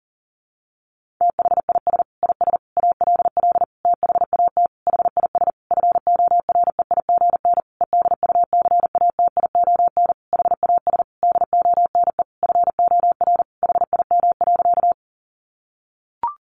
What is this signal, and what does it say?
Morse code, heard as a keyed tone, about 40 wpm, 700 Hz. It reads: THIS IS ALL THAT HIS FOREIGN EDUCATION HAS DONE FOR HIM.
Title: Sentences from Top 1400 Words 40wpm